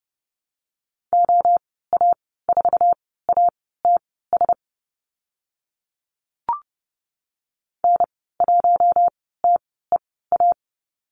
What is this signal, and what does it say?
Morse code, heard as a keyed tone, about 30 wpm, spaced as for 17 wpm, 700 Hz. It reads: OA4ATS N1TEA
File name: Lesson 14g - Number 2 - Call Signs 30wpm